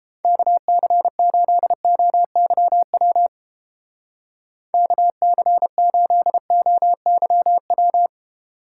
Morse code, heard as a keyed tone, about 30 wpm, 700 Hz. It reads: KC8OYW KC8OYW